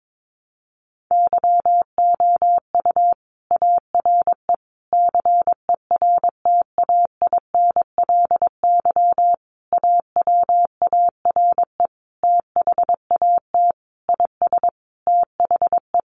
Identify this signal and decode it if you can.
Morse code, heard as a keyed tone, about 22 wpm, 700 Hz. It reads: YOU ARE CERTAINLY AWARE THAT IS THE
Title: Sentences from Top 1200 Words 22wpm